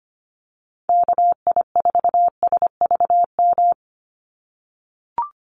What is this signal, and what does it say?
Morse code, heard as a keyed tone, about 25 wpm, 700 Hz. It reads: KI4SVM
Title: NA SOTA Activator Callsigns 25wpm